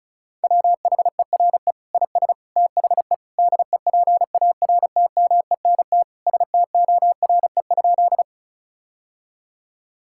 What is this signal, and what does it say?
Morse code, heard as a keyed tone, about 36 wpm, 700 Hz. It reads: WHERE IS THE DEPARTMENT STORE?
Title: Sentences from Top 1000 Words - Review Part 5 - 35wpm